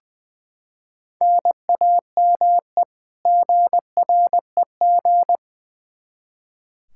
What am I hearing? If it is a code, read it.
Morse code, heard as a keyed tone, about 20 wpm, 700 Hz. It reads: NAME GREG
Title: Practice QSO - Exchange 1 - Beginning - 20wpm